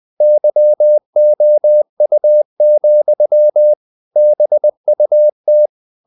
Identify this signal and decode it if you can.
Morse code, heard as a keyed tone, about 20 wpm, 600 Hz. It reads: YOU, BUT
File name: CW / Practice / If20